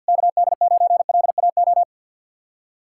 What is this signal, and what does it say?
Morse code, heard as a keyed tone, about 63 wpm, 700 Hz. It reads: KD9LAO